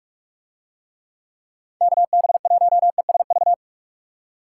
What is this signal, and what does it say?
Morse code, heard as a keyed tone, about 46 wpm, 700 Hz. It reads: KB1ESV